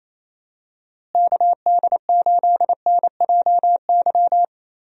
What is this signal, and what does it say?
Morse code, heard as a keyed tone, about 28 wpm, 700 Hz. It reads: KD8NJY
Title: USA Call Signs - 6 Characters - Repeated 2x - 28wpm